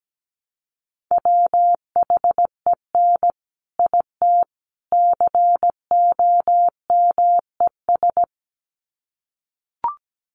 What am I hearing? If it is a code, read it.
Morse code, heard as a keyed tone, about 17 wpm, 700 Hz. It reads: WHEN IT COMES
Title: Sets of 3 Words 17wpm